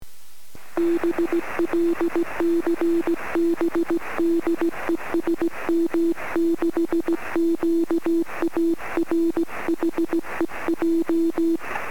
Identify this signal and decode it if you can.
Morse code, heard as a keyed tone, about 16 wpm, 330 Hz. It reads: B LCB DE SM6QA R HEJ
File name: sm7lcb / 21-sm6qa-23cm